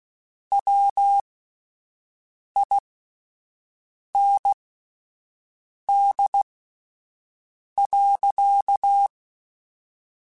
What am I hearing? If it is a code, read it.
Morse code, heard as a keyed tone, about 16 wpm, spaced as for 5 wpm, 800 Hz. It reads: WIND.
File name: Spaced Characters / Spaced Characters-mp3s / Weather sp @10